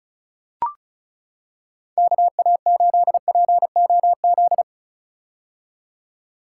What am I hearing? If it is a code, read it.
Morse code, heard as a keyed tone, about 34 wpm, 700 Hz. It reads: KA8POZ
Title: K1USN SST QSO Practice - 35wpm